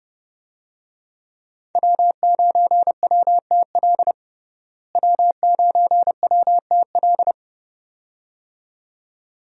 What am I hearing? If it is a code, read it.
Morse code, heard as a keyed tone, about 32 wpm, 700 Hz. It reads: W9WTL W9WTL
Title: USA Call Signs - 5 Characters - Repeated 2x - 30wpm